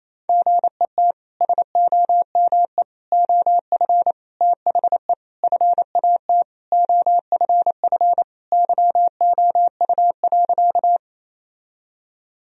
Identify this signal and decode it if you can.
Morse code, heard as a keyed tone, about 27 wpm, 700 Hz. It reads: GET SOME OF THE FAT OFF YOU.
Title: Sentences from Top 1000 Words - Review Part 4 - 28wpm